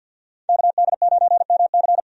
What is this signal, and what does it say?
Morse code, heard as a keyed tone, about 50 wpm, 700 Hz. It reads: KD9MC
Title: USA Call Signs - 5 Characters - 50wpm